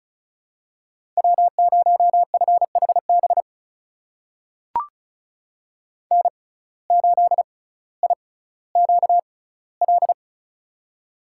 Morse code, heard as a keyed tone, about 36 wpm, spaced as for 18 wpm, 700 Hz. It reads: W0FHB N8IQL
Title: USA Call Signs - 5 Characters - 6x Character Spacing - 35wpm